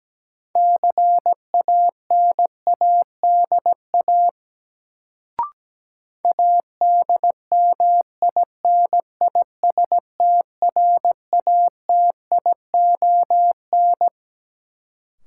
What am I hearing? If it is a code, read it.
Morse code, heard as a keyed tone, about 17 wpm, 700 Hz. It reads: CANADA ADMINISTRATION
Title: Sets of 5 Words - New Word Drill - 17wpm